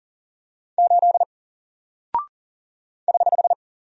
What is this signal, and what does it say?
Morse code, heard as a keyed tone, about 39 wpm, 700 Hz. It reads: 8 <HH>
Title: Single Character (Letter:Number:Punctuation:Prosign) 40wpm